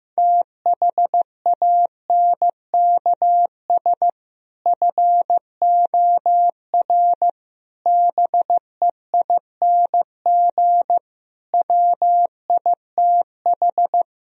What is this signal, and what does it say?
Morse code, heard as a keyed tone, about 15 wpm, 700 Hz. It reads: THANKS FOR BEING WITH
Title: Speed-Racing - Sets of 5 Words 15wpm